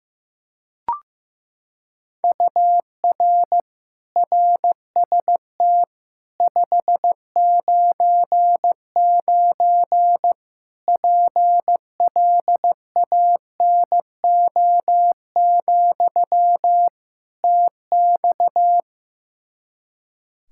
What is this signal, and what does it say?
Morse code, heard as a keyed tone, about 15 wpm, 700 Hz. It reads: UR RST 599 PLANO, TX
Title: K1USN SST QSO Practice 15wpm